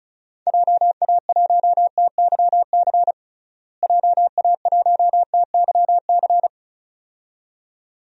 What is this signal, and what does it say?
Morse code, heard as a keyed tone, about 36 wpm, 700 Hz. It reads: JA1TYC JA1TYC
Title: DXCC-100 Call Signs - 6 Characters - Repeated 2x - 35wpm